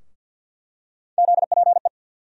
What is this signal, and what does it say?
Morse code, heard as a keyed tone, about 50 wpm, 700 Hz. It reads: ZPE
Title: USA Call Sign Postfixes - 3 Letters 50wpm